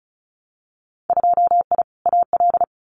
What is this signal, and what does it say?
Morse code, heard as a keyed tone, about 36 wpm, 700 Hz. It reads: 2I AL
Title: Winter Field Day Exchanges 35wpm